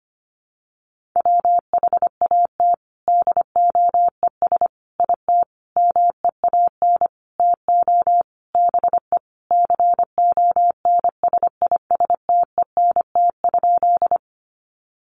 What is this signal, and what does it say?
Morse code, heard as a keyed tone, about 25 wpm, 700 Hz. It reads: WHAT DOES IT MEAN TO BE CONSISTENT?
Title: Sentences from Top 1500 Words 25wpm